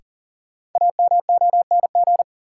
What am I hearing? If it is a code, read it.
Morse code, heard as a keyed tone, about 40 wpm, 700 Hz. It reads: AMONG
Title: Sentences from Top 500 Words - New Word Drill - 40wpm